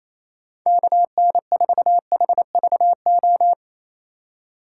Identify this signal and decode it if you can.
Morse code, heard as a keyed tone, about 28 wpm, 700 Hz. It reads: KN4HVO
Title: USA Call Signs - 6 Characters - 3x Character Spacing - 28wpm